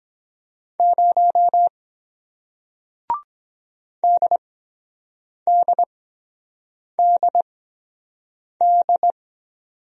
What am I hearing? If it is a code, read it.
Morse code, heard as a keyed tone, about 23 wpm, spaced as for 7 wpm, 700 Hz. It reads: 0 DDDD